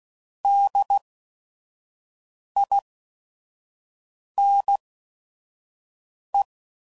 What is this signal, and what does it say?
Morse code, heard as a keyed tone, about 16 wpm, spaced as for 5 wpm, 800 Hz. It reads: DINE
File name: Prefixes / By First Letter / Ds